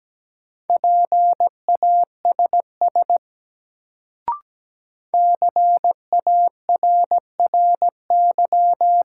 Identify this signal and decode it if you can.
Morse code, heard as a keyed tone, about 17 wpm, 700 Hz. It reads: PASS CARRY
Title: Sentences from Top 400 Words - New Word Drill - 17wpm